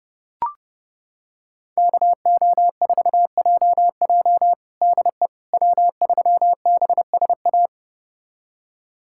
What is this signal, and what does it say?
Morse code, heard as a keyed tone, about 31 wpm, 700 Hz. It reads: KO4JJ DE W3BSA
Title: USA Callsign DE Callsign - 5 Characters - 30wpm